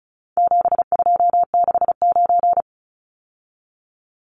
Morse code, heard as a keyed tone, about 34 wpm, 700 Hz. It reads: 7269
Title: Numbers - 4-Digit 35wpm